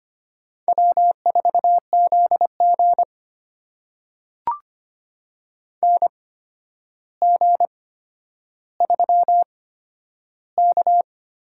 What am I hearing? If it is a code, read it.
Morse code, heard as a keyed tone, about 25 wpm, spaced as for 11 wpm, 700 Hz. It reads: W4ZG NG3K
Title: USA Call Signs - 4 Characters - 8x Character Spacing - 25wpm